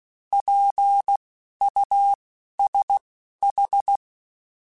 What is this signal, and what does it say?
Morse code, heard as a keyed tone, about 16 wpm, spaced as for 11 wpm, 800 Hz. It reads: PUSH
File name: Alphabetical / Alphabetical mp3s / Ps @10